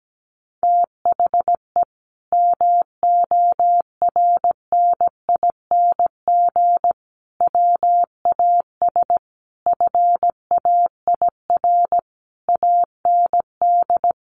Morse code, heard as a keyed tone, about 17 wpm, 700 Hz. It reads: THE MORNING WAS FAIR AND
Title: Sentences from Top 1300 Words 17wpm